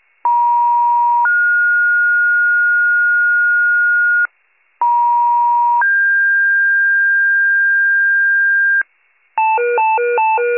Two-tone page, heard as a one-second tone, then a three-second tone, about 950 then 1430 Hz, about 950 then 1600 Hz, with noise. It is heard from 0.2 to 8.8 s.